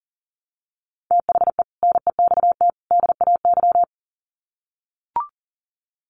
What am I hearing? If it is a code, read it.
Morse code, heard as a keyed tone, about 40 wpm, 700 Hz. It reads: THE NEXT DAY
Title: Sets of 3 Words 40wpm